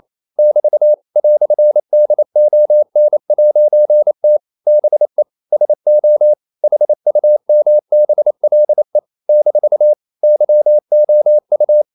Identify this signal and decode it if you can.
Morse code, heard as a keyed tone, about 28 wpm, 600 Hz. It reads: <BT> "DON'T BE SO HUMBLE - YOU
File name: CW / Practice / EnglishQuotations0128